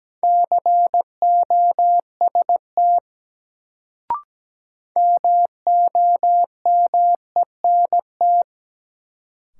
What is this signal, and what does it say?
Morse code, heard as a keyed tone, about 17 wpm, 700 Hz. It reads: COST MOMENT
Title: Sets of 3 Words - New Word Drill - 17wpm